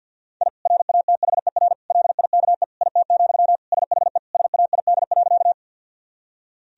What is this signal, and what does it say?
Morse code, heard as a keyed tone, about 50 wpm, 700 Hz. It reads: I RATHER LIKE IT, SHE SAID.
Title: Sentences from Top 1000 Words - Review Part 4 - 50wpm